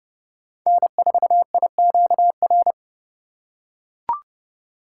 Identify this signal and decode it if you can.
Morse code, heard as a keyed tone, about 30 wpm, 700 Hz. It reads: N4IQR
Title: USA Call Signs - 5 Characters - 3x Character Spacing - 30wpm